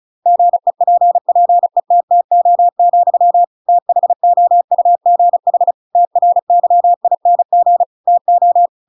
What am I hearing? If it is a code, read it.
Morse code, heard as a keyed tone, about 34 wpm, 700 Hz. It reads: GEPPETTO, THOUGH TRYING TO
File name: Pinocchio / Pinocchio ch8 35 35 ik1hge